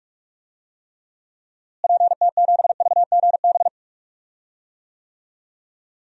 Morse code, heard as a keyed tone, about 41 wpm, 700 Hz. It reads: PT7VGB
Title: DXCC-100 Call Signs - 6 Characters - Repeated 2x - 45wpm